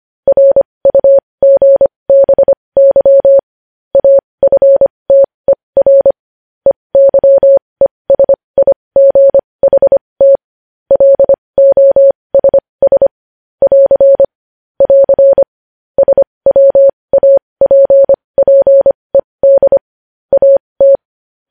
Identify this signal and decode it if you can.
Morse code, heard as a keyed tone, about 25 wpm, spaced as for 20 wpm, 550 Hz. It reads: RUGBY AFTER EYESIGHT LOSS <AR> <AR> SWAPPED AT
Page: News headlines in morse code at 20 wpm